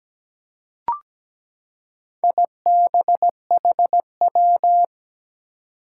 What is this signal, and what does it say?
Morse code, heard as a keyed tone, about 17 wpm, 700 Hz. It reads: IBHW
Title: Four Challenging Letter-Number Combinations 17wpm